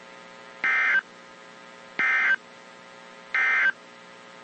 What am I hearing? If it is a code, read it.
Emergency Alert System (SAME) tones, heard as harsh buzzing bursts. It is heard from 0.5 to 3.8 s.
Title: WZ2523 - NOAA Weather Radio